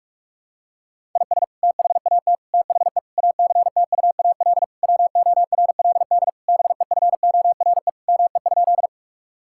Morse code, heard as a keyed tone, about 46 wpm, 700 Hz. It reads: IS THAT THE ACTUAL WORLD BEFORE ME?
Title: Sentences from Top 1000 Words - Review Part 5 - 45wpm